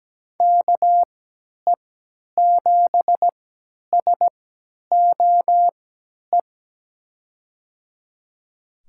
Morse code, heard as a keyed tone, about 17 wpm, spaced as for 10 wpm, 700 Hz. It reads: KE7SOE
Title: USA Call Signs - 6 Characters - 3x Character Spacing - 17wpm